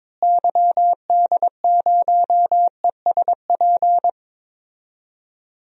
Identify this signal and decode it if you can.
Morse code, heard as a keyed tone, about 22 wpm, 700 Hz. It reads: YD0ESP
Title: DXCC-100 Call Signs - 6 Characters - Repeated 2x - 22wpm